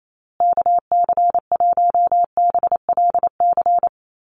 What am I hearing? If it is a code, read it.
Morse code, heard as a keyed tone, about 28 wpm, 700 Hz. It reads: KC1BLC
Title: USA Call Signs Repeated 3x 28wpm